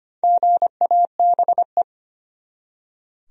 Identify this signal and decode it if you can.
Morse code, heard as a keyed tone, about 25 wpm, 700 Hz. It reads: GABE
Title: USA Common Names - Max 4 Letters - 25wpm